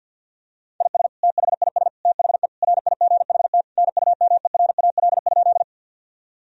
Morse code, heard as a keyed tone, about 50 wpm, 700 Hz. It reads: IS THIS THE RIGHT NUMERAL?
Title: Sentences from Top 1000 Words - Review Part 2 - 50wpm